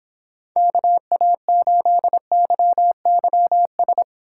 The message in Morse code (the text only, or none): KA8YYS